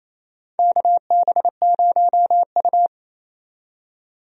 Morse code, heard as a keyed tone, about 28 wpm, 700 Hz. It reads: KB0U
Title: USA Call Signs - 4 Characters - 28wpm